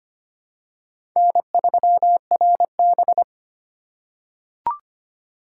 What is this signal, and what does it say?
Morse code, heard as a keyed tone, about 25 wpm, 700 Hz. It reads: N3RB
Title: USA Call Signs - 4 Characters - 6x Character Spacing - 25wpm